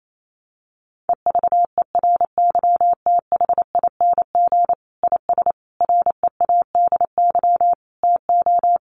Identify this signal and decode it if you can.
Morse code, heard as a keyed tone, about 28 wpm, 700 Hz. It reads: EVERYTHING IS READY TO
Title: Sentences from Top 700 Words 28wpm